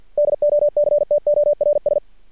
Morse code, heard as a keyed tone, about 50 wpm, 590 Hz. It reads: DOCTORS
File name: archived / CRSnet